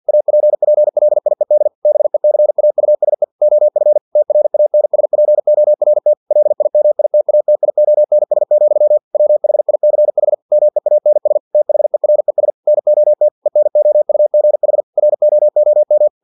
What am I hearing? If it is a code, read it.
Morse code, heard as a keyed tone, about 50 wpm, 600 Hz. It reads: APPLIED BECAUSE OF TRANSPORT LIMITATIONS, WHICH MEANS THERES NOT ENOUGH ROOM
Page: Never 49wpm